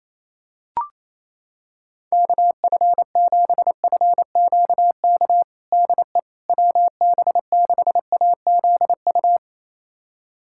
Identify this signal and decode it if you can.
Morse code, heard as a keyed tone, about 28 wpm, 700 Hz. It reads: KF7FQK DE WB6AZU
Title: USA Callsign DE Callsign - 6 Characters - 28wpm